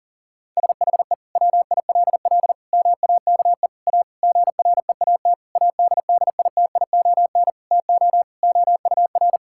Morse code, heard as a keyed tone, about 40 wpm, 700 Hz. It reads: SHE WILL MAKE A GREAT ADDITION TO OUR
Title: Sentences from Top 1000 Words - Review Part 6 - 40wpm